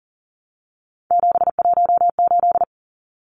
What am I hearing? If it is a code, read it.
Morse code, heard as a keyed tone, about 40 wpm, 700 Hz. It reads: 718
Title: Numbers - 3-Digit 40wpm